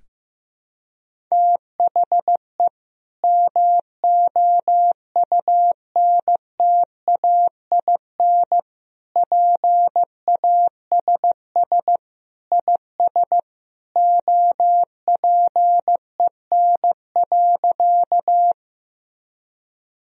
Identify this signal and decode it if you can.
Morse code, heard as a keyed tone, about 15 wpm, 700 Hz. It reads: THE MOUNTAIN PASS IS OPEN.
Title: Sentences from Top 1000 Words - Review Part 1 - 15wpm